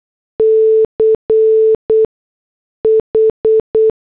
Morse code, heard as a keyed tone, about 8 wpm, spaced as for 6 wpm, 440 Hz. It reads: CH